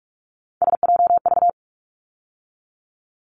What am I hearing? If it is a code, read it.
Morse code, heard as a keyed tone, about 46 wpm, 700 Hz. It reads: SJV
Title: ARRL Sections 45wpm